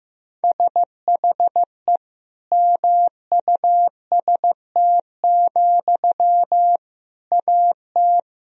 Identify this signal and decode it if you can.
Morse code, heard as a keyed tone, about 15 wpm, 700 Hz. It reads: SHE MUST, AT
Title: Sentences from Top 1000 Words - Review Part 4 - 15wpm